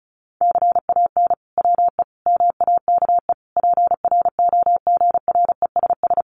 Morse code, heard as a keyed tone, about 34 wpm, 700 Hz. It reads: CAN WE MAKE PROGRESS
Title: Sentences from Top 1500 Words 35wpm